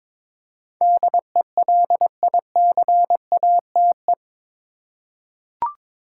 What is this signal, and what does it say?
Morse code, heard as a keyed tone, about 22 wpm, 700 Hz. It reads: DELICATE
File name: practice / 22wpm / Eight Letter Words - 22wpm